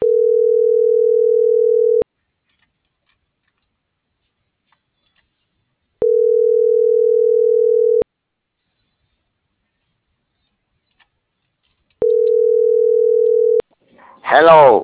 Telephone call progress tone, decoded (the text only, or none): ringback tone